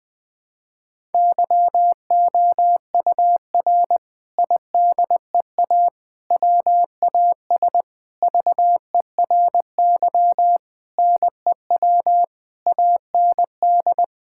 Morse code, heard as a keyed tone, about 20 wpm, 700 Hz. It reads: YOUR IDEA WAS VERY NEW AND
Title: Sentences from Top 1000 Words - Review Part 6 - 20wpm